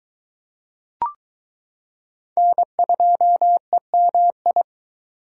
Morse code, heard as a keyed tone, about 22 wpm, 700 Hz. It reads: N2EMI